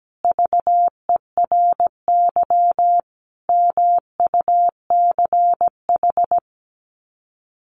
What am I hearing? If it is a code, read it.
Morse code, heard as a keyed tone, about 17 wpm, 700 Hz. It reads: VERY MUCH
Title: Sets of 5 Words Encore 17wpm